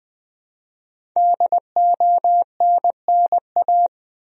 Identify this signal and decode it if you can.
Morse code, heard as a keyed tone, about 21 wpm, 700 Hz. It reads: DONNA
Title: USA Common Names - 20wpm